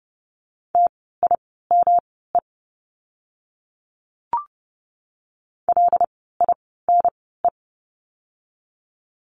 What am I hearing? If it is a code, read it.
Morse code, heard as a keyed tone, about 30 wpm, spaced as for 17 wpm, 700 Hz. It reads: TIME LINE
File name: course / 30:10wpm / 115 Lesson 19f - Letter w - Words 30wpm